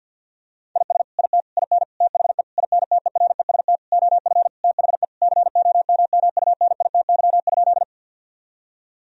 Morse code, heard as a keyed tone, about 50 wpm, 700 Hz. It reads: IS IT IN THE INTEREST OF THE COMMUNITY?
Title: Sentences from Top 1000 Words - Review Part 4 - 50wpm